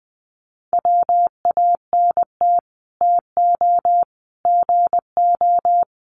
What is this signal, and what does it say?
Morse code, heard as a keyed tone, about 20 wpm, 700 Hz. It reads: WANT TO GO
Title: Sets of 5 Words 20wpm